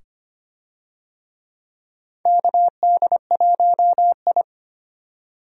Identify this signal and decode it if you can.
Morse code, heard as a keyed tone, about 25 wpm, 700 Hz. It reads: KD1I
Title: USA Call Signs - 4 Characters - 25wpm